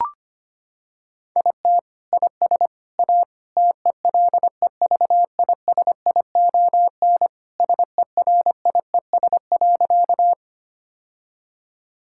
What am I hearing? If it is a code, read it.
Morse code, heard as a keyed tone, about 25 wpm, 700 Hz. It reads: IT IS A TELEVISION SERIES.
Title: Sentences from Top 1000 Words - Review Part 5 - 25wpm